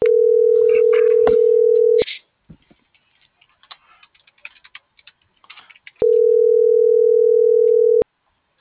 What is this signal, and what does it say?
Ringback tone (North America), 440 + 480 Hz.